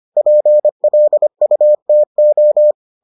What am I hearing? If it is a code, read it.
Morse code, heard as a keyed tone, about 25 wpm, 600 Hz. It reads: PLUTO